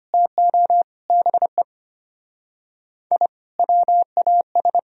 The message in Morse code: TO BE I WAS

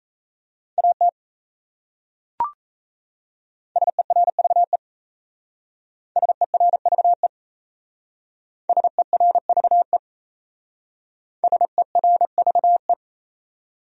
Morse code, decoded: AT SERVE SERVE SERVE SERVE

